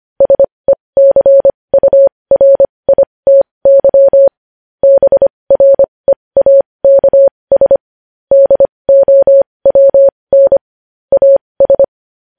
Morse code, heard as a keyed tone, about 25 wpm, spaced as for 20 wpm, 550 Hz. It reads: SECURITY BREAKS DOWN AS
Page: news-headlines-in-morse-code-at-20-wpm